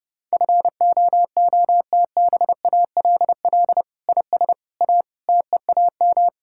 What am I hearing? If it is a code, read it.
Morse code, heard as a keyed tone, about 30 wpm, 700 Hz. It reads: FOOTBALL IS A TEAM
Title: Sentences from Top 1600 Words 30wpm